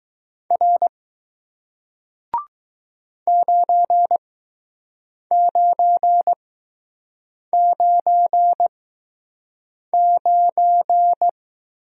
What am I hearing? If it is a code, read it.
Morse code, heard as a keyed tone, about 21 wpm, spaced as for 6 wpm, 700 Hz. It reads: R 9999